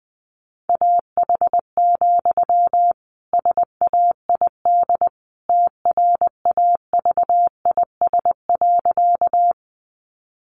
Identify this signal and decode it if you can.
Morse code, heard as a keyed tone, about 20 wpm, 700 Hz. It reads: AH, SAID TRAVIS.
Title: A Sound Of Thunder 20wpm